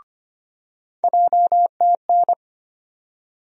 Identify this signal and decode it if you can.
Morse code, heard as a keyed tone, about 25 wpm, 700 Hz. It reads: JTN
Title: USA Call Sign Postfixes - 3 Letters 25wpm